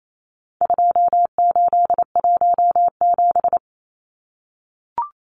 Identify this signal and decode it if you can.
Morse code, heard as a keyed tone, about 28 wpm, 700 Hz. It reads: 2817
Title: Numbers - 4-Digit 28wpm